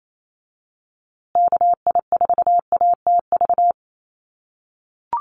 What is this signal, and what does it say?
Morse code, heard as a keyed tone, about 28 wpm, 700 Hz. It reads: KI4ATV
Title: USA Call Signs Repeated 3x 28wpm